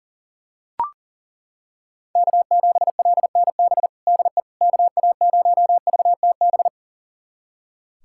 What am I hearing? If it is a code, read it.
Morse code, heard as a keyed tone, about 40 wpm, 700 Hz. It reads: K7LNB DE KA0VTB
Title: USA Callsign DE Callsign 40wpm